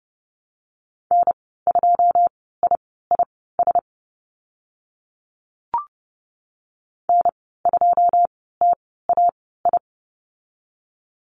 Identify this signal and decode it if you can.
Morse code, heard as a keyed tone, about 30 wpm, spaced as for 17 wpm, 700 Hz. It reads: N2IIS N2TAI
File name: Lesson 15g - Number 5 - Call Signs 30wpm